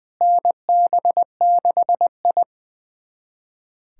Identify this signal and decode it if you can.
Morse code, heard as a keyed tone, about 20 wpm, 700 Hz. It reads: NB6I
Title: USA Callsign DE Callsign - 4 Characters - 20wpm